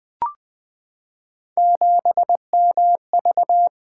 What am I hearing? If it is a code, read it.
Morse code, heard as a keyed tone, about 20 wpm, 700 Hz. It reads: 7MV
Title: Three Letter-Number Combinations - 20wpm